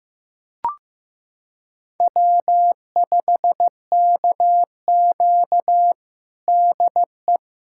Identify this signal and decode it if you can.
Morse code, heard as a keyed tone, about 15 wpm, 700 Hz. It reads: W5KQ DE
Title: USA Callsign DE Callsign - 4 Characters - 15wpm